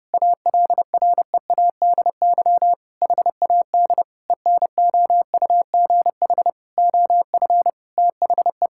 Morse code, heard as a keyed tone, about 30 wpm, 700 Hz. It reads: ALREADY HAD ENOUGH OF THE
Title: Sentences from Top 1600 Words 30wpm